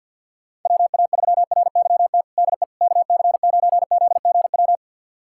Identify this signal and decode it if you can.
Morse code, heard as a keyed tone, about 52 wpm, 700 Hz. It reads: WA3RYT DE KC9ZGW
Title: USA Callsign DE Callsign 50wpm